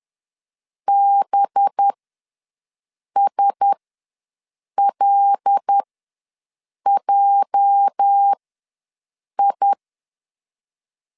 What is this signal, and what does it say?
Morse code, heard as a keyed tone, about 10 wpm, spaced as for 6 wpm, 800 Hz. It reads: BSLJI
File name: LEZIONI / 8 YJ45 / 8 YJ ws 10wpm 4043